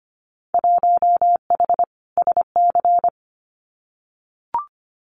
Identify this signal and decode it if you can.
Morse code, heard as a keyed tone, about 25 wpm, 700 Hz. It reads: 1H SC
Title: Winter Field Day Exchanges 25wpm